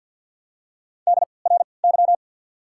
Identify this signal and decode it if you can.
Morse code, heard as a keyed tone, about 53 wpm, spaced as for 29 wpm, 700 Hz. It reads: DRY
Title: 3x Character Spacing - Top 500 Words - 50wpm